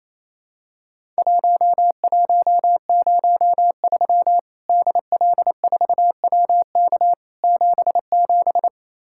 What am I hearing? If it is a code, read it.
Morse code, heard as a keyed tone, about 28 wpm, 700 Hz. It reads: 1103 DL4WK 77
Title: Worked All Europe DX Contest - QTC Exchange - 28wpm